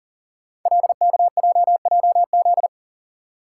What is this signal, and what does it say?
Morse code, heard as a keyed tone, about 40 wpm, 700 Hz. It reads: LKJJZ
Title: Code Groups - 5 Letters - 40wpm